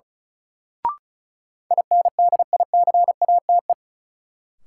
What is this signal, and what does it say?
Morse code, heard as a keyed tone, about 34 wpm, 700 Hz. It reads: INDICATE
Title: Sentences from Top 1300 Words - New Word Drill Alt - 35wpm